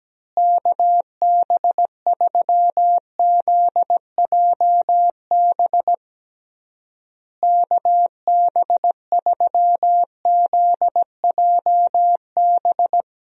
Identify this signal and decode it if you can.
Morse code, heard as a keyed tone, about 17 wpm, 700 Hz. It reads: KB3ZJB KB3ZJB
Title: USA Call Signs - 6 Characters - Repeated 3x - 17wpm